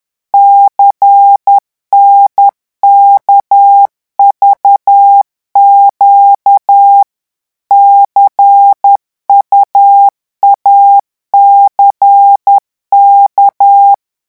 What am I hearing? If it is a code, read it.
Morse code, heard as a keyed tone, about 10 wpm, 800 Hz. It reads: CNKVQ CUACK
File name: morse mp3 / revision3-vitesse 10 mots